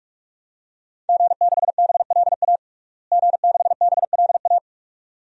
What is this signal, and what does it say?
Morse code, heard as a keyed tone, about 42 wpm, 700 Hz. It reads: G6BLA G6BLA
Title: DXCC-100 Call Signs - 5 Characters - Repeated 2x - 45wpm